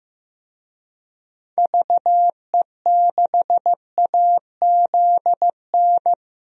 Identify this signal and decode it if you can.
Morse code, heard as a keyed tone, about 16 wpm, 700 Hz. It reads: VE6AZN
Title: DXCC-100 Call Signs - 6 Characters - 15wpm